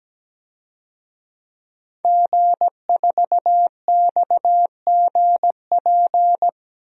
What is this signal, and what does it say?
Morse code, heard as a keyed tone, about 17 wpm, 700 Hz. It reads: G4XGP
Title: DXCC-100 Call Signs - 5 Characters - 17wpm